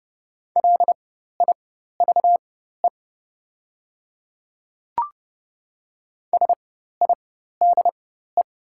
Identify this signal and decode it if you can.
Morse code, heard as a keyed tone, about 30 wpm, spaced as for 14 wpm, 700 Hz. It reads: LIVE SIDE
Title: 4x Character Spacing - Top 300 Words - 30wpm